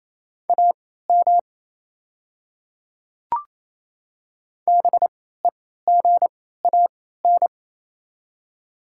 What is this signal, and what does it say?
Morse code, heard as a keyed tone, about 28 wpm, spaced as for 16 wpm, 690 Hz. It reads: AM BEGAN
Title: 3x Character Spacing - Top 500 Words - 28wpm